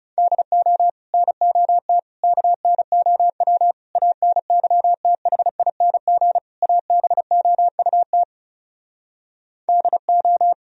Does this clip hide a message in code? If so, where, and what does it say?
Yes: Morse code from 0.2 to 10.5 s, heard as a keyed tone, about 40 wpm, 700 Hz. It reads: DO NOT KNOW ANYTHING ABOUT DO